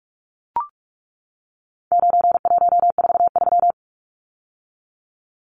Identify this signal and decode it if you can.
Morse code, heard as a keyed tone, about 44 wpm, 700 Hz. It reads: 9143
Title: Numbers - 4-Digit 45wpm